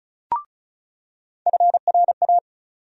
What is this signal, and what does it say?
Morse code, heard as a keyed tone, about 34 wpm, 700 Hz. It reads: FRA
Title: USA Call Sign Postfixes - 3 Letters 35wpm